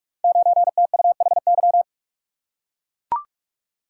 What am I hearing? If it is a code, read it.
Morse code, heard as a keyed tone, about 44 wpm, 700 Hz. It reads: 9TUHY
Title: Code Groups - 5 Letter:Numbers - 45wpm